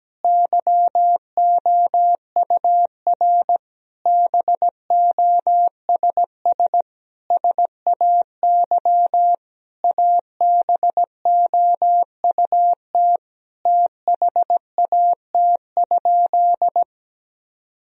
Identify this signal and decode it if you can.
Morse code, heard as a keyed tone, about 17 wpm, 700 Hz. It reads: YOUR BOSS SAY ABOUT THAT?
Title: Sentences from Top 1000 Words - Review Part 4 - 17wpm